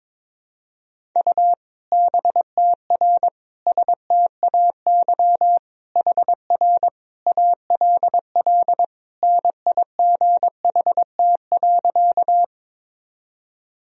Morse code, heard as a keyed tone, about 21 wpm, 700 Hz. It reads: U BTR STAY HR ALL NIGHT.
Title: Sentences from Top 500 Words and Abbreviations 22wpm